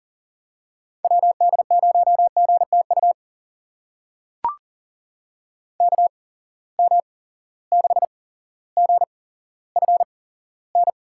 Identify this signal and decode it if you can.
Morse code, heard as a keyed tone, about 40 wpm, spaced as for 17 wpm, 700 Hz. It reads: WD0GTU KM6GFN
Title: USA Call Signs - 6 Characters - 8x Character Spacing - 40wpm